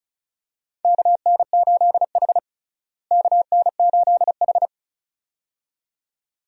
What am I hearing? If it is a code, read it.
Morse code, heard as a keyed tone, about 34 wpm, 700 Hz. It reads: KN8H KN8H
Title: USA Call Signs - 4 Characters - Repeated 2x - 35wpm